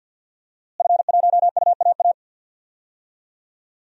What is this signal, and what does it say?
Morse code, heard as a keyed tone, about 46 wpm, 700 Hz. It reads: F1UAA
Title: DXCC-100 Call Signs - 5 Characters - Repeated 3x - 50wpm